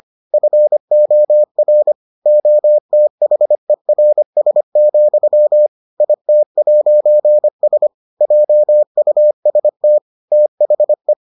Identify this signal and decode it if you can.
Morse code, heard as a keyed tone, about 25 wpm, 600 Hz. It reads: FOR OTHERS, IT'S JUST THE